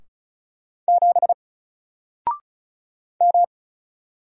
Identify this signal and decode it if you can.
Morse code, heard as a keyed tone, about 36 wpm, 700 Hz. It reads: 7 M
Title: Single Character (Letter:Number:Punctuation:Prosign) Alt 35wpm